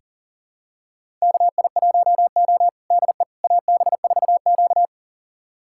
Morse code, heard as a keyed tone, about 40 wpm, 700 Hz. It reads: KI1O DE AB4Q
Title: USA Callsign DE Callsign - 4 Characters - 40wpm